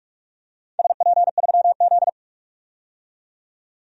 Morse code, heard as a keyed tone, about 44 wpm, 700 Hz. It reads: SP3Z
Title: DXCC-100 Call Signs - 4 Characters - 45wpm